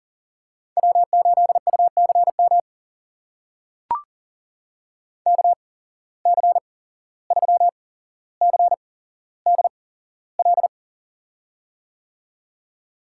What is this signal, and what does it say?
Morse code, heard as a keyed tone, about 40 wpm, spaced as for 16 wpm, 700 Hz. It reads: W8UCM KC3CDL
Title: USA Call Signs - 8x Character Spacing - 40wpm